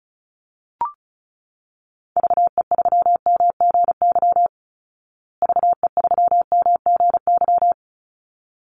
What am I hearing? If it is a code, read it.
Morse code, heard as a keyed tone, about 34 wpm, 700 Hz. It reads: VE3MGY VE3MGY
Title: Call Signs Repeated 3x - 35wpm